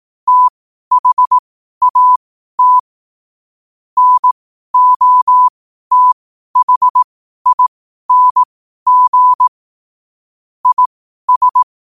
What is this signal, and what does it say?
Morse code, heard as a keyed tone, about 18 wpm, spaced as for 12 wpm, 1000 Hz. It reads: THAT NOTHING IS
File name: qotdfiles / 12wpm